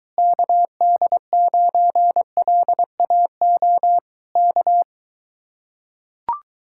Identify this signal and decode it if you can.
Morse code, heard as a keyed tone, about 23 wpm, 700 Hz. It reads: KD9LAO K